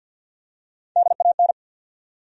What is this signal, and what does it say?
Morse code, heard as a keyed tone, about 50 wpm, 700 Hz. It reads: DAN
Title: USA Common Names - Max 4 Letters - 50wpm